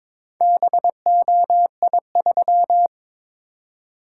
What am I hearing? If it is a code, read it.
Morse code, heard as a keyed tone, about 22 wpm, 700 Hz. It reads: BOI3
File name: practice / 22wpm / Four Letter-Number Combinations - 22wpm